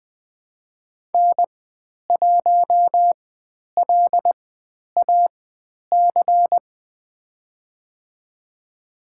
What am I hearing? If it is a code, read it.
Morse code, heard as a keyed tone, about 20 wpm, spaced as for 10 wpm, 700 Hz. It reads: N1LAC
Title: 102 Lesson 17g - Letter u - Call Signs 20wpm